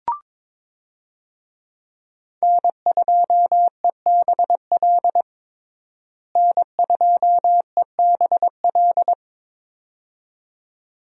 Morse code, heard as a keyed tone, about 22 wpm, 700 Hz. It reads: N2EBL N2EBL
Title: USA Call Signs - 5 Characters - Repeated 2x - 22wpm